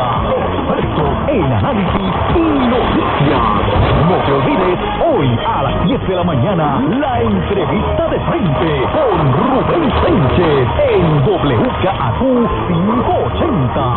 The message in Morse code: none

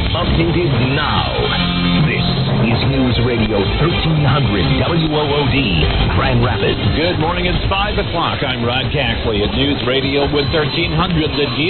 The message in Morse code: none

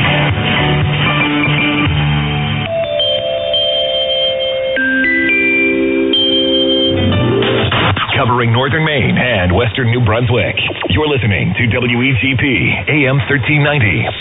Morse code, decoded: none

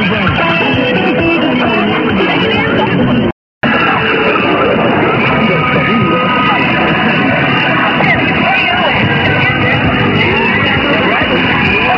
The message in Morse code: none

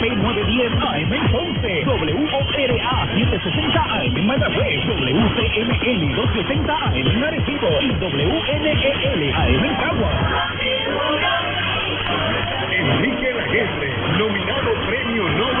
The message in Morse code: none

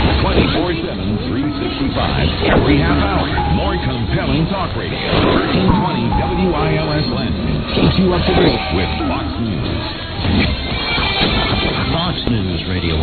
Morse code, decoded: none

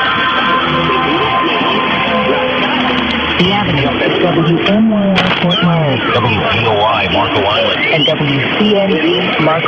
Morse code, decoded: none

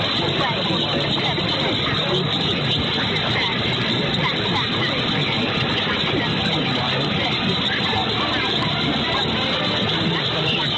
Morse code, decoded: none

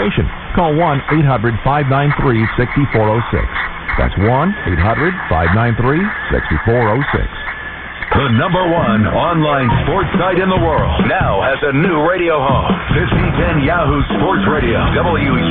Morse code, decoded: none

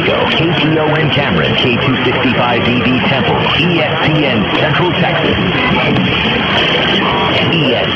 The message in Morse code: none

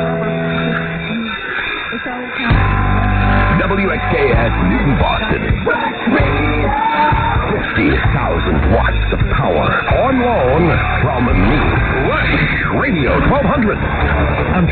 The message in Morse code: none

